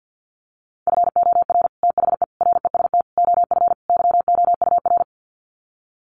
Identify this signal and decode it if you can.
Morse code, heard as a keyed tone, about 50 wpm, 700 Hz. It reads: FOR THE REST OF YOUR
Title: Sets of 5 Words Encore 50wpm